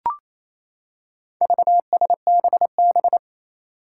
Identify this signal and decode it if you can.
Morse code, heard as a keyed tone, about 28 wpm, 700 Hz. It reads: VSBB
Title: Four Challenging Letter-Number Combinations 28wpm